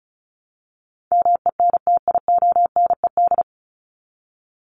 Morse code, heard as a keyed tone, about 34 wpm, 700 Hz. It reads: MENTIONED
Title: ED Suffix - Top 100 Words with Base Words 35wpm